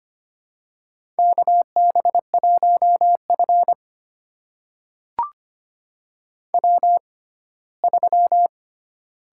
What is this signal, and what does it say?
Morse code, heard as a keyed tone, about 25 wpm, spaced as for 17 wpm, 700 Hz. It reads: KB1F W3
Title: USA Call Signs - 4 Characters - 6x Character Spacing - 25wpm